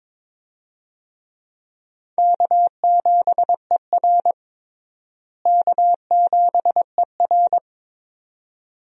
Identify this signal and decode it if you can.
Morse code, heard as a keyed tone, about 22 wpm, 700 Hz. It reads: K7ER K7ER
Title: USA Call Signs - 4 Characters - Repeated 2x - 22wpm